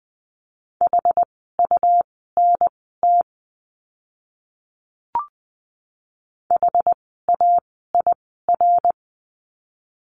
Morse code, heard as a keyed tone, about 20 wpm, spaced as for 15 wpm, 700 Hz. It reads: HUNT HAIR
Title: 101 Lesson 17f - Letter u - Words 20wpm